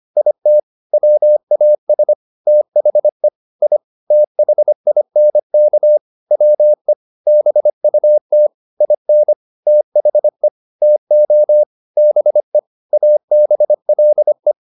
Morse code, heard as a keyed tone, about 25 wpm, 600 Hz. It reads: IT WAS THE I THINK WE BUT IN THE TO BE ABLE